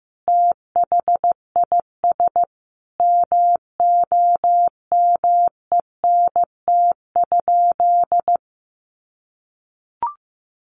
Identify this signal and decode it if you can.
Morse code, heard as a keyed tone, about 15 wpm, 700 Hz. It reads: THIS MOMENT?
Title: Sentences from Top 1000 Words 15wpm